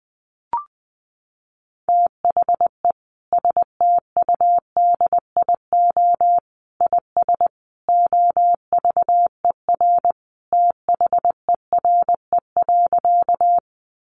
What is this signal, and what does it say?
Morse code, heard as a keyed tone, about 20 wpm, 700 Hz. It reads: THE STUDIO IS OVER THERE.
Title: Sentences from Top 1200 Words 20wpm